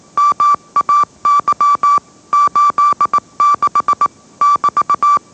Morse code, heard as a keyed tone, about 18 wpm, 1160 Hz. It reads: MAY 86 <BT>
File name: mrx / mrx081 20wpm